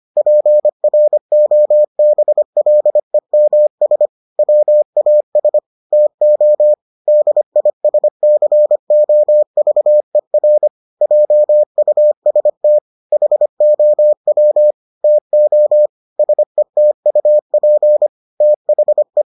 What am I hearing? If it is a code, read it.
Morse code, heard as a keyed tone, about 25 wpm, 600 Hz. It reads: PROBLEMS WAS TO DISCOVER JUST HOW TO SETUP THE